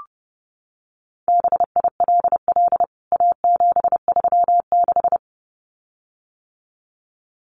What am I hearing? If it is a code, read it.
Morse code, heard as a keyed tone, about 30 wpm, 700 Hz. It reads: BILL A736
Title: CWT Exchange 30wpm